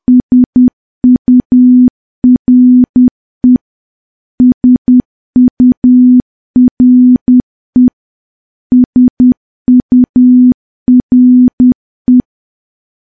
Morse code, heard as a keyed tone, about 10 wpm, 250 Hz. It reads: SURE SURE SURE
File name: morse / morse-training-10WPM